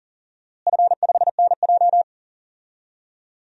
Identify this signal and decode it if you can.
Morse code, heard as a keyed tone, about 40 wpm, 700 Hz. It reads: F5NJ
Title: DXCC-100 Call Signs - 4 Characters - Repeated 3x - 40wpm